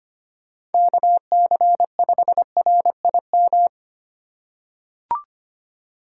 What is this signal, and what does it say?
Morse code, heard as a keyed tone, about 25 wpm, 700 Hz. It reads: KC5RIM
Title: USA Call Signs - 6 Characters - 4x Character Spacing - 25wpm